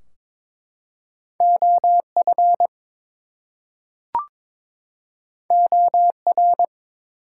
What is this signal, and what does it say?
Morse code, heard as a keyed tone, about 22 wpm, 700 Hz. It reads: OF OR
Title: Two Letter Words - 22wpm